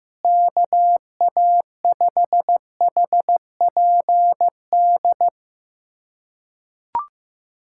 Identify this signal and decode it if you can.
Morse code, heard as a keyed tone, about 15 wpm, 700 Hz. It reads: KA5HPD
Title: USA Callsign DE Callsign - 6 Characters - 15wpm